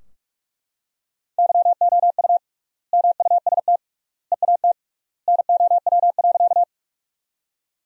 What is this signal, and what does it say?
Morse code, heard as a keyed tone, about 44 wpm, 700 Hz. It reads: YOU MUST EAT NOW.
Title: 3x Word-Spacing - Sentences from Top 300 Words - 45wpm